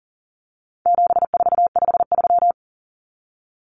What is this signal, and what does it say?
Morse code, heard as a keyed tone, about 40 wpm, 700 Hz. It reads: 7453
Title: Numbers - 4-Digit 40wpm